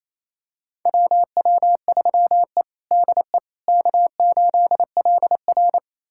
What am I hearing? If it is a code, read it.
Morse code, heard as a keyed tone, about 28 wpm, 700 Hz. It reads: WW3E DE K8LR
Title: USA Callsign DE Callsign - 4 Characters - 28wpm